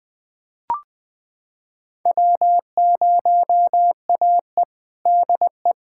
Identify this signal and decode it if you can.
Morse code, heard as a keyed tone, about 20 wpm, 700 Hz. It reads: W0AE DE
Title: USA Callsign DE Callsign - 4 Characters - 20wpm